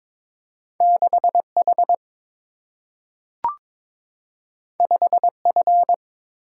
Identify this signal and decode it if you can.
Morse code, heard as a keyed tone, about 23 wpm, 700 Hz. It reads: 6 H 5 F
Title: Single-Letter Single-Number 22wpm